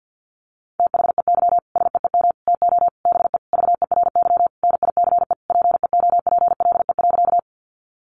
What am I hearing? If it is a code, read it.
Morse code, heard as a keyed tone, about 48 wpm, 700 Hz. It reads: THEY SEEM TO BE VERY NICE PEOPLE.
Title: Sentences from Top 1100 Words 50wpm